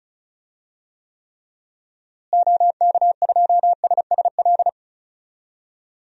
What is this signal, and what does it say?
Morse code, heard as a keyed tone, about 36 wpm, 700 Hz. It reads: OK2SSL
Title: DXCC-100 Call Signs - 6 Characters - Repeated 2x - 35wpm